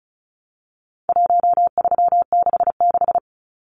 Morse code, heard as a keyed tone, about 34 wpm, 700 Hz. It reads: 1366
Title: Numbers - Contest Style - 4-Digit 35wpm